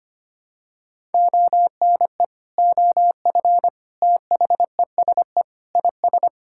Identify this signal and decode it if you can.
Morse code, heard as a keyed tone, about 25 wpm, 700 Hz. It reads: ONE OF THESE IS
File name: practice / 25wpm / Sentences with 2-Word Phrases - 25wpm